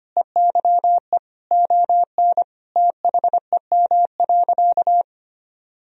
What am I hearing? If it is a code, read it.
Morse code, heard as a keyed tone, about 25 wpm, 700 Hz. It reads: EYE ON THEM.
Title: Sentences from Top 1000 Words - Review Part 1 - 25wpm